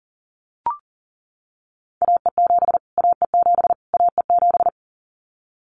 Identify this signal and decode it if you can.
Morse code, heard as a keyed tone, about 40 wpm, 700 Hz. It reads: AE7 AE7 AE7
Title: Callsign Prefixes for Top 25 DXCC - Repeated 3x - 40wpm